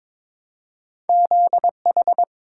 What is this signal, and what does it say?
Morse code, heard as a keyed tone, about 22 wpm, 700 Hz. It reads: ZH
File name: practice / 22wpm / Two Letter-Number Combinations - 22wpm